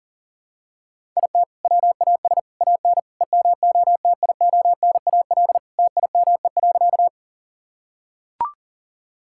Morse code, heard as a keyed tone, about 40 wpm, 700 Hz. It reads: IT WAS AN EMOTIONAL TIME.
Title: Sentences from Top 1000 Words - Review Part 5 - 40wpm